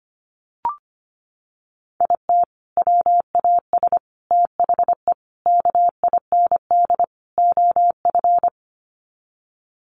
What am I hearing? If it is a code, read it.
Morse code, heard as a keyed tone, about 25 wpm, 700 Hz. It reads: IT WAS THE KIND OF
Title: Sets of 5 Words Encore2 25wpm